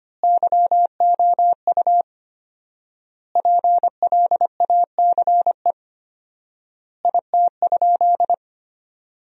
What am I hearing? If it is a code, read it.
Morse code, heard as a keyed tone, about 25 wpm, 700 Hz. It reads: YOU PLACE IT?
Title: 4x Word-Spacing - Sentences from Top 200 Words - 25wpm